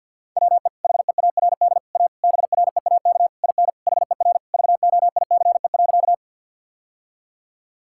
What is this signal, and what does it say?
Morse code, heard as a keyed tone, about 50 wpm, 700 Hz. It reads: WE HEARD A BREAK IN HER VOICE.
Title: Sentences from Top 1000 Words - Review Part 6 - 50wpm